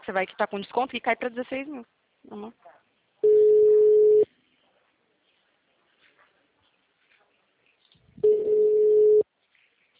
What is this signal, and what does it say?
Ringback tone (Europe, 425 Hz), with speech.
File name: out- -2025- - - .49